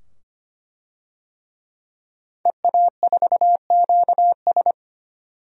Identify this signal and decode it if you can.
Morse code, heard as a keyed tone, about 25 wpm, 700 Hz. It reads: EA4QS
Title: DXCC-100 Call Signs - 5 Characters - 25wpm